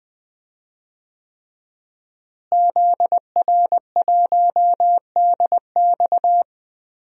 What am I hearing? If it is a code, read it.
Morse code, heard as a keyed tone, about 20 wpm, 700 Hz. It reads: ZR1DX
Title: DXCC-100 Call Signs - 5 Characters - Repeated 3x - 20wpm